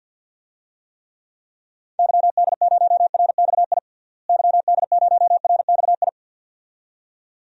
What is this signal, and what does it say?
Morse code, heard as a keyed tone, about 52 wpm, 700 Hz. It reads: YD0RXI YD0RXI
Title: DXCC-100 Call Signs - 6 Characters - Repeated 2x - 50wpm